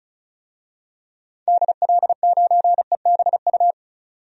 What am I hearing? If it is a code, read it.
Morse code, heard as a keyed tone, about 34 wpm, 700 Hz. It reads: DL9EBU
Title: DXCC-100 Call Signs - 6 Characters - 35wpm